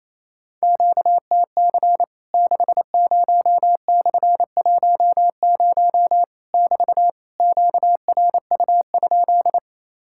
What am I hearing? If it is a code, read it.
Morse code, heard as a keyed tone, about 28 wpm, 700 Hz. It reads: QTC 60/10 <BT> QRU?
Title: Worked All Europe DX Contest - QTC Exchange - 28wpm